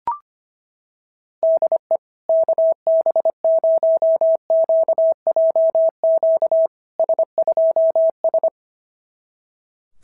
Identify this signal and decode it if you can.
Morse code, heard as a keyed tone, about 25 wpm, 660 Hz. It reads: DE KB0QJQ S2S
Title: SOTA QSO Practice 25wpm